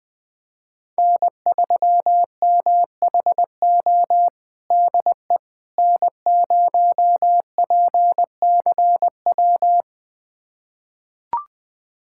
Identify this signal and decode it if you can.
Morse code, heard as a keyed tone, about 21 wpm, 700 Hz. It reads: N3MHO DE N0PCW
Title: USA Callsign DE Callsign - 5 Characters - 20wpm